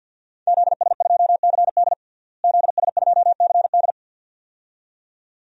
Morse code, heard as a keyed tone, about 52 wpm, 700 Hz. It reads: ZS2CD ZS2CD
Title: DXCC-100 Call Signs - 5 Characters - Repeated 2x - 50wpm